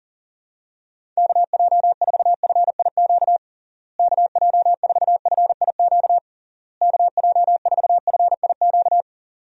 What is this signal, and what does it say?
Morse code, heard as a keyed tone, about 40 wpm, 700 Hz. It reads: KJ4FIQ KJ4FIQ KJ4FIQ
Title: USA Call Signs - 6 Characters - Repeated 3x - 40wpm